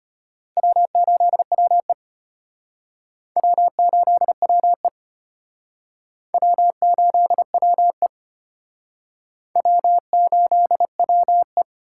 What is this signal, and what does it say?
Morse code, heard as a keyed tone, about 34 wpm, 700 Hz. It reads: W8WE W8WE W8WE W8WE